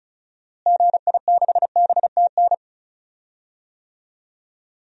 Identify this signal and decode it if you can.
Morse code, heard as a keyed tone, about 36 wpm, 700 Hz. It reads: GI6BTN
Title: DXCC-100 Call Signs - 6 Characters - Repeated 3x - 35wpm